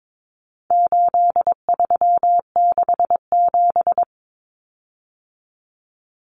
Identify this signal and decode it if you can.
Morse code, heard as a keyed tone, about 22 wpm, 700 Hz. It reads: 8367
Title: Common Cut Numbers - 4-Digit 22wpm